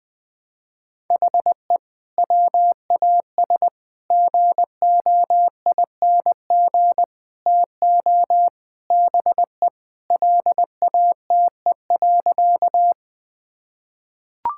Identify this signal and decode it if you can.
Morse code, heard as a keyed tone, about 20 wpm, 700 Hz. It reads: HE WAS GOING TO BE LATE.
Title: Sentences with 5-Word Phrases - 20wpm